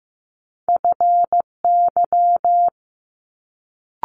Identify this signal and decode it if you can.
Morse code, heard as a keyed tone, about 15 wpm, 700 Hz. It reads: FY
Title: Multi-letter Phonemes Quick Fire 15wpm